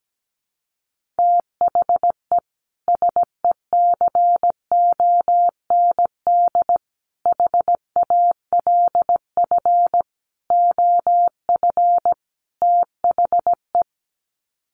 Morse code, heard as a keyed tone, about 17 wpm, 700 Hz. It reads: THE SECOND HALF OF THE
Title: Sets of 5 Words 17wpm